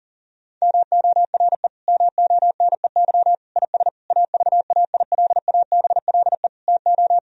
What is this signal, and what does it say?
Morse code, heard as a keyed tone, about 40 wpm, 700 Hz. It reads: MORE MONEY IS AVAILABLE TO
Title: Sentences from Top 1000 Words - Review Part 3 - 40wpm